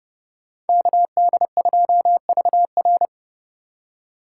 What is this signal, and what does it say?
Morse code, heard as a keyed tone, about 28 wpm, 700 Hz. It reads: KD2VR